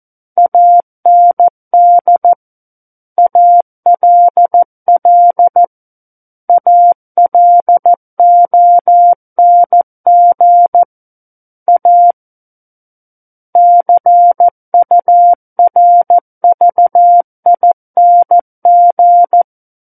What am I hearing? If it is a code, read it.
Morse code, heard as a keyed tone, about 14 wpm, 700 Hz. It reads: AND ALL ALONG A CURVING
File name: Book One / 05-Chapter 5 - 14 WPM